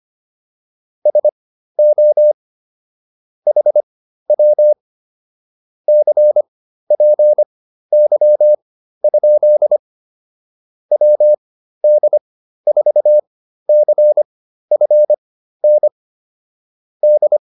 Morse code, heard as a keyed tone, about 25 wpm, spaced as for 13 wpm, 600 Hz. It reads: SO HW CPY? WD4CFN D
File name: QSO 207 13